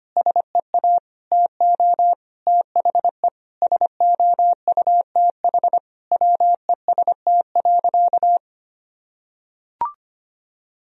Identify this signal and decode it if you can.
Morse code, heard as a keyed tone, about 25 wpm, 700 Hz. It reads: SEA TO THE SOUTH WEST.
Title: Sentences from Top 1000 Words - Review Part 2 - 25wpm